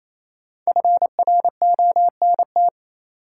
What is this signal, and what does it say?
Morse code, heard as a keyed tone, about 28 wpm, 700 Hz. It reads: FRONT
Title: Sentences from Top 500 Words - New Word Drill - 28wpm